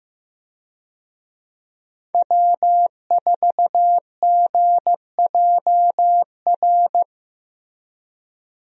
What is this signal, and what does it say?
Morse code, heard as a keyed tone, about 15 wpm, 700 Hz. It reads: W4GJR
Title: USA Call Signs - 5 Characters - Repeated 3x - 15wpm